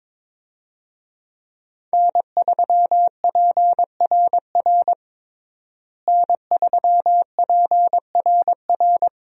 Morse code, heard as a keyed tone, about 22 wpm, 700 Hz. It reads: N3PRR N3PRR
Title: USA Call Signs - 5 Characters - Repeated 3x - 22wpm